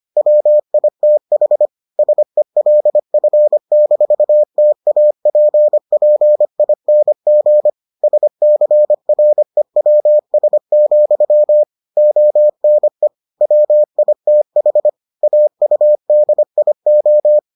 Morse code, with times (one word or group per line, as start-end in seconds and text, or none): 0.2-1.7 WITH
2.0-7.7 SELF-TAPPING
8.0-11.6 SCREWS,
12.0-13.1 ONE
13.4-14.9 WITH
15.2-17.4 AUDIO